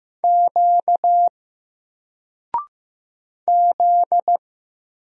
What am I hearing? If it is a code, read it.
Morse code, heard as a keyed tone, about 15 wpm, 700 Hz. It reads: Q Z
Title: Single Character (Letter:Number:Punctuation:Prosign) 15wpm